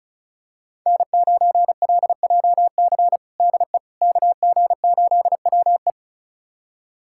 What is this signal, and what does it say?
Morse code, heard as a keyed tone, about 36 wpm, 700 Hz. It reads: N9LJC DE KG8WE
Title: USA Callsign DE Callsign - 5 Characters - 35wpm